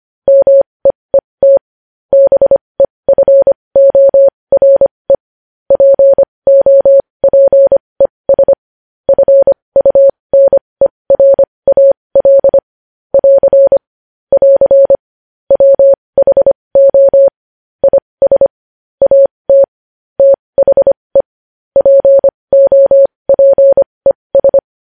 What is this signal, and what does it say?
Morse code, heard as a keyed tone, about 25 wpm, spaced as for 20 wpm, 550 Hz. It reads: MEET BEFORE POPES FUNERAL <AR> <AR> WHO IS AT THE POPES
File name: podcast / News headlines-1x-25x20